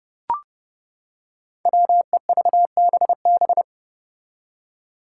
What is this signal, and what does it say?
Morse code, heard as a keyed tone, about 30 wpm, 700 Hz. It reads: WEVBB
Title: Five Challenging Letter-Number Combinations 30wpm